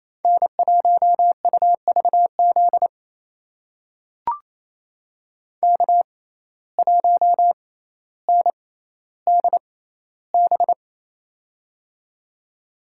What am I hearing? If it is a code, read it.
Morse code, heard as a keyed tone, about 28 wpm, spaced as for 14 wpm, 700 Hz. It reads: N1UVZ K1NDB
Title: USA Call Signs - 5 Characters - 6x Character Spacing - 28wpm